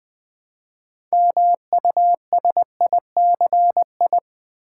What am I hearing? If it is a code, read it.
Morse code, heard as a keyed tone, about 20 wpm, 700 Hz. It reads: MUSICI
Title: Build A Word - 2 Letters At A Time - 8-Letter Words - 20wpm